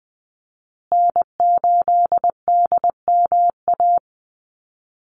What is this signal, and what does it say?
Morse code, heard as a keyed tone, about 20 wpm, 700 Hz. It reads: N8DMA
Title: USA Call Signs Repeated 3x 20wpm